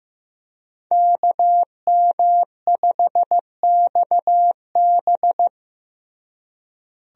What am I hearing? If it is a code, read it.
Morse code, heard as a keyed tone, about 15 wpm, 700 Hz. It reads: KM5XB
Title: USA Call Signs - 5 Characters - Repeated 3x - 15wpm